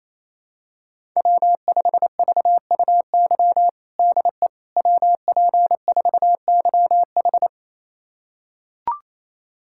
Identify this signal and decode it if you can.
Morse code, heard as a keyed tone, about 27 wpm, 700 Hz. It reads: W5VUY DE WP4YH
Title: USA Callsign DE Callsign - 5 Characters - 28wpm